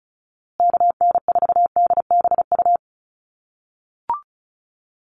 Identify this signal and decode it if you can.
Morse code, heard as a keyed tone, about 34 wpm, 700 Hz. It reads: KN4DBU
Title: USA Call Signs Repeated 3x 35wpm